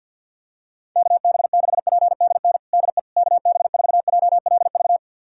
Morse code, heard as a keyed tone, about 46 wpm, 700 Hz. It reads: KB6PDN DE KB4JLV
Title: USA Callsign DE Callsign - 6 Characters - 50wpm